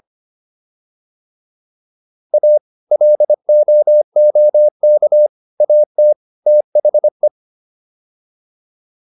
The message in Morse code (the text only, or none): A LOOK AT THE